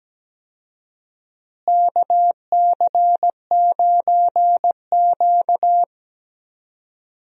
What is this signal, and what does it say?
Morse code, heard as a keyed tone, about 17 wpm, 700 Hz. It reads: KC9Q
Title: USA Call Signs - 4 Characters - Repeated 3x - 17wpm